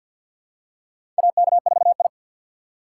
Morse code, heard as a keyed tone, about 46 wpm, 700 Hz. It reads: AK4I
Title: USA Call Signs - 4 Characters - 3x Character Spacing - 50wpm